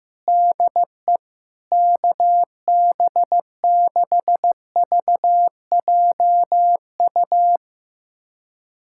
Morse code, heard as a keyed tone, about 15 wpm, 700 Hz. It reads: DE KB6VJU
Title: USA Callsign DE Callsign - 6 Characters - 15wpm